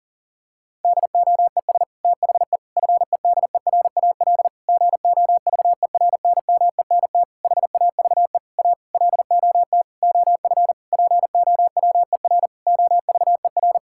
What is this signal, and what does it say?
Morse code, heard as a keyed tone, about 40 wpm, 700 Hz. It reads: DOES THE FEDERAL GOVERNMENT HAVE A LOT OF POWER OVER
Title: Sentences from Top 1000 Words - Review Part 5 - 40wpm